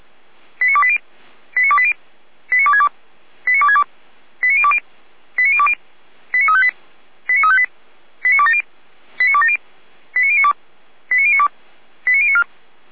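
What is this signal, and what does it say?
Five-tone selective call (ZVEI), heard as quick runs of five tones, tones of about 70 ms. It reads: 89280 89280 89262 89262 89020 89020 89358 89358 89270 89270 89092 89092 89094
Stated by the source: Five-tone ZVEI selective call received by SDR.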